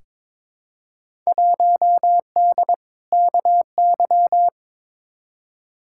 Morse code, heard as a keyed tone, about 22 wpm, 700 Hz. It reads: 1D KY
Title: Field Day Exchanges - 22wpm